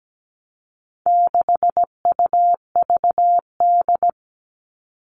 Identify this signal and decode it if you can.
Morse code, heard as a keyed tone, about 17 wpm, 700 Hz. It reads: 6UVD
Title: All Cut Numbers - 4-Digit 17wpm